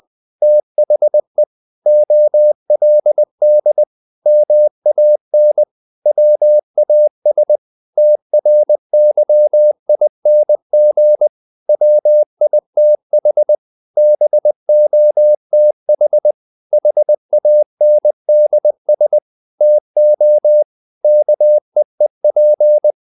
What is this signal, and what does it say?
Morse code, heard as a keyed tone, about 20 wpm, 600 Hz. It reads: THE OLD MAN WAS TRYING WITH BOTH HANDS TO KEEP
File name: the old man and the sea / the old man and the sea part 1 20wpm